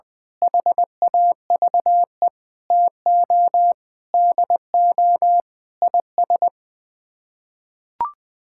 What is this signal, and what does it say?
Morse code, heard as a keyed tone, about 21 wpm, 700 Hz. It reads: HAVE TO DO IS
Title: Sets of 6 Words - 20wpm